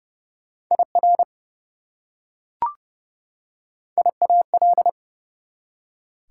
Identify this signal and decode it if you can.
Morse code, heard as a keyed tone, about 32 wpm, 700 Hz. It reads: IR IAL
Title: Prefixes and Suffixes 30wpm